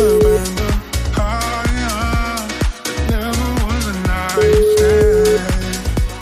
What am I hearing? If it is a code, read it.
Ringback tone (Europe, 425 Hz), with noise.